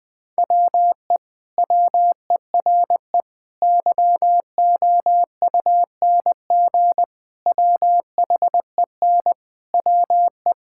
Morse code, heard as a keyed tone, about 20 wpm, 700 Hz. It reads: WE WERE YOUNG WHEN WE
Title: Sentences with 2-Word Phrases - Spoken Beginning - 20wpm